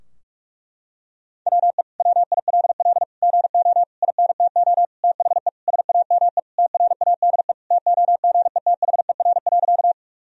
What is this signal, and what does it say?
Morse code, heard as a keyed tone, about 41 wpm, 700 Hz. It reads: WE WILL GO INTO THE SAME TRADE TOGETHER.
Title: Sentences from Top 1000 Words - Review Part 4 - 45wpm